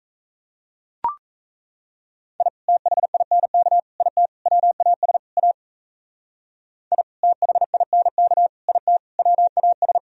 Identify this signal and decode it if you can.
Morse code, heard as a keyed tone, about 34 wpm, 700 Hz. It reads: I THINK IT WAS A I THINK IT WAS
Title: Speed-Racing - Sets of 5 Words 28wpm